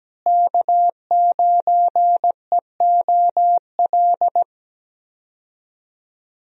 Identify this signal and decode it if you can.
Morse code, heard as a keyed tone, about 17 wpm, 700 Hz. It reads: K9EOL
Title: USA Call Signs - 5 Characters - Repeated 2x - 17wpm